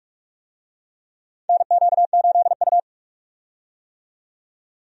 Morse code, heard as a keyed tone, about 48 wpm, 700 Hz. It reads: NQ8U
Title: USA Call Signs - 4 Characters - 45wpm